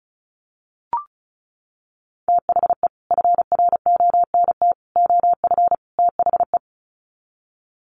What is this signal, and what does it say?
Morse code, heard as a keyed tone, about 34 wpm, 700 Hz. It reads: THE FRONT OF THE
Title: Sets of 4 Words 35wpm